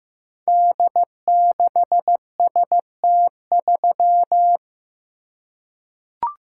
Morse code, heard as a keyed tone, about 15 wpm, 700 Hz. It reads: D6ST3
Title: Code Groups - 5 Letter:Numbers - 15wpm